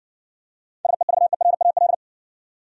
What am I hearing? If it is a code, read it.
Morse code, heard as a keyed tone, about 56 wpm, 700 Hz. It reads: SEVERAL